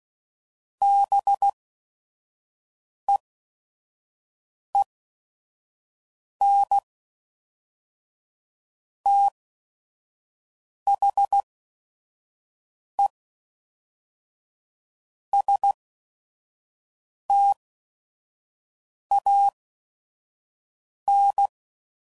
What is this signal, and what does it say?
Morse code, heard as a keyed tone, about 16 wpm, spaced as for 5 wpm, 800 Hz. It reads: BEEN THE STAN
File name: CW Code Groups @5WPM / Technical Subjects / AND Gate